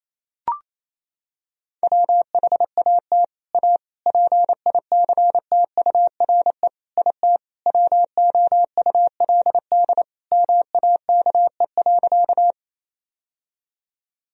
Morse code, heard as a keyed tone, about 28 wpm, 700 Hz. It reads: WHAT A PICTURE IT WOULD MAKE.
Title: Sentences from Top 1000 Words - Review Part 1 - 28wpm